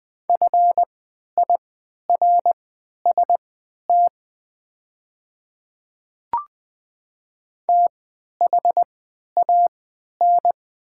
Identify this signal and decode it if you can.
Morse code, heard as a keyed tone, about 20 wpm, spaced as for 11 wpm, 700 Hz. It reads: FIRST THAN
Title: 3x Character Spacing - Top 100 Words - 20wpm